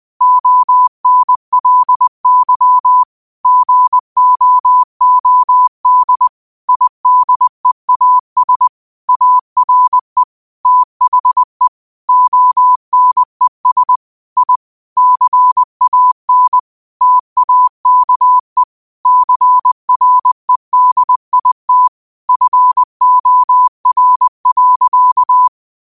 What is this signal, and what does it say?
Morse code, heard as a keyed tone, about 20 wpm, 1000 Hz. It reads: ONLY GOOD IDEAS ARE THE ONES I CAN TAKE CREDIT FOR.